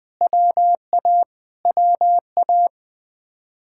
Morse code, heard as a keyed tone, about 20 wpm, 700 Hz. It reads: WA WA